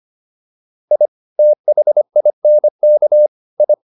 Morse code, heard as a keyed tone, about 25 wpm, 600 Hz. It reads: I THINK I